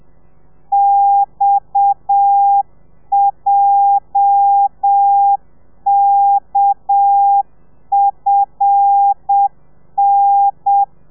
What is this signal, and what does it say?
Morse code, heard as a keyed tone, about 7 wpm, 800 Hz. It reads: XJKFN
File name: telegrafia / i2tfj / Lez8 35